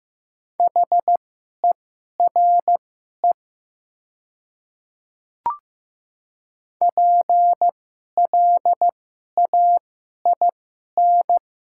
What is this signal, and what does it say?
Morse code, heard as a keyed tone, about 15 wpm, spaced as for 11 wpm, 700 Hz. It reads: HERE PLAIN
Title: 2x Character Spacing - Top 500 Words - 15wpm